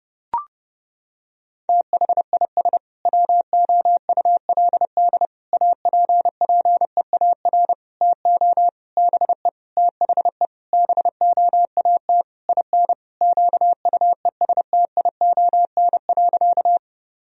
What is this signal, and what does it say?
Morse code, heard as a keyed tone, about 31 wpm, 700 Hz. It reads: THIS WOULD APPEAR TO BE THE BOAT IN QUESTION.
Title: Sentences from Top 1000 Words - Review Part 3 - 30wpm